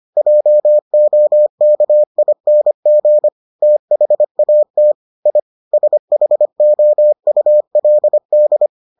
Morse code, heard as a keyed tone, about 25 wpm, 600 Hz. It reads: JOKING THAT I SHOULD